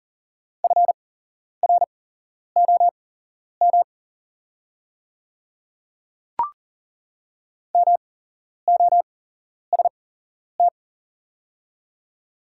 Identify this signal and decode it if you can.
Morse code, heard as a keyed tone, about 40 wpm, spaced as for 11 wpm, 690 Hz. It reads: FROM MOST
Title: 8x Character Spacing - Top 100 Words - 40wpm